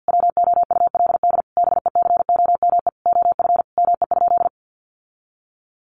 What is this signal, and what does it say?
Morse code, heard as a keyed tone, about 50 wpm, 700 Hz. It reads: WOULD BECOME OF ME?
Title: Sentences from Top 600 Words 50wpm